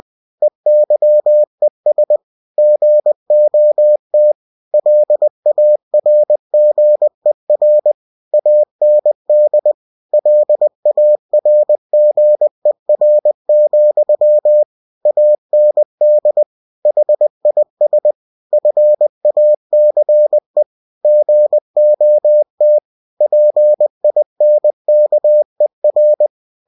Morse code, heard as a keyed tone, about 20 wpm, 600 Hz. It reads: EYES GOT LARGER AND LARGER, AND HIS FACE GOT PINKER
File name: cw-books / winnie-the-pooh / winnie-20wpm-page-0004